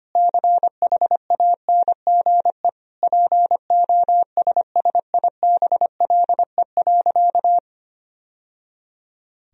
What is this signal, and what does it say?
Morse code, heard as a keyed tone, about 25 wpm, 700 Hz. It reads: CHANGE POSSIBLE.
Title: Sentences from Top 1000 Words - Review Part 1 - 25wpm